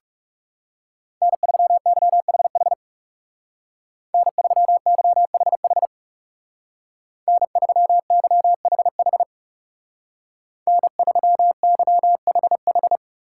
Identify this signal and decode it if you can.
Morse code, heard as a keyed tone, about 35 wpm, 700 Hz. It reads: N3YHH N3YHH N3YHH N3YHH
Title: Speed Racing - USA 1x3 Call Signs 30wpm